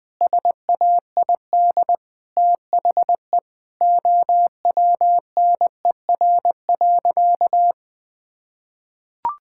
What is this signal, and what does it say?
Morse code, heard as a keyed tone, about 20 wpm, 700 Hz. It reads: SAID THE OWNER.
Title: Sentences from Top 1600 Words 20wpm